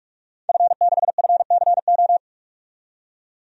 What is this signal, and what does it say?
Morse code, heard as a keyed tone, about 46 wpm, 700 Hz. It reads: F6FCO
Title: DXCC-100 Call Signs - 5 Characters - 45wpm